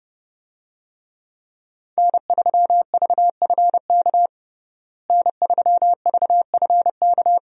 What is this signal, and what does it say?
Morse code, heard as a keyed tone, about 30 wpm, 700 Hz. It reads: N3VFK N3VFK
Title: USA Call Signs - 5 Characters - Repeated 2x - 30wpm